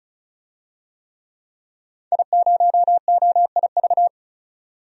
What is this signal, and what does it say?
Morse code, heard as a keyed tone, about 36 wpm, 700 Hz. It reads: I0OIV
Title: DXCC-100 Call Signs - 5 Characters - 35wpm